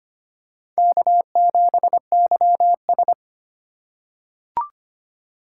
Morse code, heard as a keyed tone, about 25 wpm, 700 Hz. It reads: K7YS
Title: USA Call Signs - 4 Characters - 4x Character Spacing - 25wpm